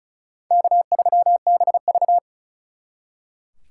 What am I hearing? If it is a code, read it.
Morse code, heard as a keyed tone, about 45 wpm, 700 Hz. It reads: K3BV